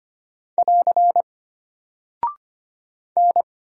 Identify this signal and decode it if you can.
Morse code, heard as a keyed tone, about 25 wpm, 700 Hz. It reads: <AR> N
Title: Single Character (Letter:Number:Punctuation:Prosign) Alt 25wpm